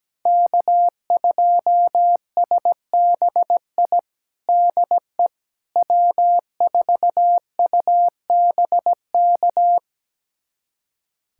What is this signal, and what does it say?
Morse code, heard as a keyed tone, about 17 wpm, 700 Hz. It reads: K2SBI DE W4UBK
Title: USA Callsign DE Callsign - 5 Characters - 17wpm